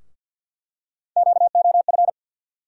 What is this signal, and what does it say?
Morse code, heard as a keyed tone, about 47 wpm, 700 Hz. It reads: QOF
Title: USA Call Sign Postfixes - 3 Letters 50wpm